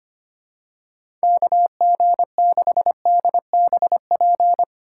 Morse code, heard as a keyed tone, about 25 wpm, 700 Hz. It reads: KG6DBP
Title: USA Call Signs - 6 Characters - 2x Character Spacing - 25wpm